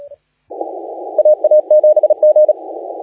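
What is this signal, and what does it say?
Morse code, heard as a keyed tone, about 36 wpm, 610 Hz. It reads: AA7G
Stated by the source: Morse contest audio recorded by LA4C during the SAC CW 2018 contest.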